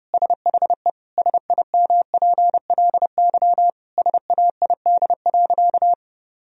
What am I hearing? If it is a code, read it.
Morse code, heard as a keyed tone, about 30 wpm, 700 Hz. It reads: SHE SIMPLY SAID.
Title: Sentences from Top 1000 Words - Review Part 4 - 30wpm